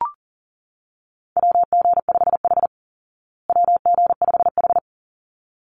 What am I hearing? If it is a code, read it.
Morse code, heard as a keyed tone, about 40 wpm, 700 Hz. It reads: WG5H WG5H
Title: Call Signs Repeated 3x - 40wpm